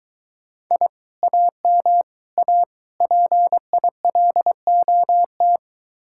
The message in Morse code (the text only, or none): I AM A PILOT